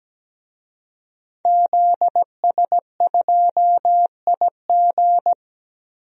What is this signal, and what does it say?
Morse code, heard as a keyed tone, about 17 wpm, 700 Hz. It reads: ZS2IG
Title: DXCC-100 Call Signs - 5 Characters - Repeated 3x - 17wpm